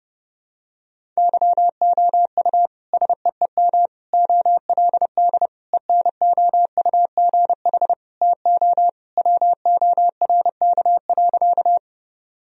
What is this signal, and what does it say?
Morse code, heard as a keyed tone, about 30 wpm, 700 Hz. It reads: YOU SEEM OLD ENOUGH TO WORK.
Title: Sentences from Top 1000 Words - Review Part 1 - 30wpm